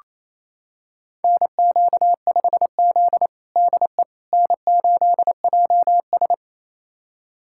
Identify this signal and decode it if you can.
Morse code, heard as a keyed tone, about 28 wpm, 700 Hz. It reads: NQ5Z DE N8JS
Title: USA Callsign DE Callsign - 4 Characters - 28wpm